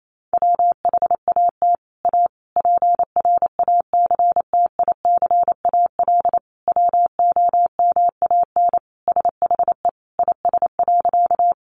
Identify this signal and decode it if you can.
Morse code, heard as a keyed tone, about 28 wpm, 700 Hz. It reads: WHAT A PRACTICAL WOMAN SHE IS.
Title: Sentences from Top 1400 Words 28wpm